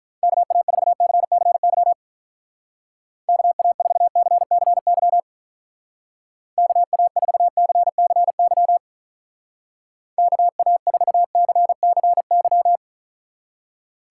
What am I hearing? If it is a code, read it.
Morse code, heard as a keyed tone, about 44 wpm, 700 Hz. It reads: KA4CCY KA4CCY KA4CCY KA4CCY